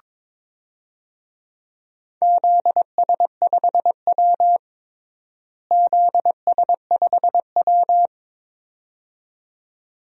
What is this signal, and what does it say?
Morse code, heard as a keyed tone, about 22 wpm, 700 Hz. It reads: ZS5W ZS5W
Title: DXCC-100 Call Signs - 4 Characters - Repeated 2x - 22wpm